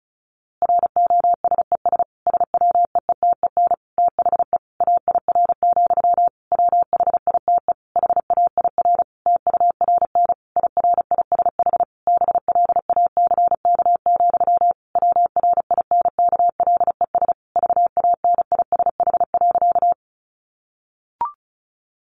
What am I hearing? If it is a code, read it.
Morse code, heard as a keyed tone, about 34 wpm, 700 Hz. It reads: ROSES SWEETEN THE AIR, WHITE HAIR TURN IRISH BLACK, WRINKLES VANISH.
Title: A Sound Of Thunder 35wpm